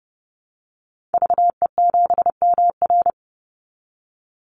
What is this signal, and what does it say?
Morse code, heard as a keyed tone, about 30 wpm, 700 Hz. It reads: VE7MR
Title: Call Signs Repeated 2x - 30wpm